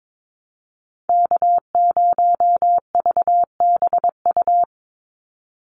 Morse code, heard as a keyed tone, about 22 wpm, 700 Hz. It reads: K0VBU
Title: Call Signs - 22wpm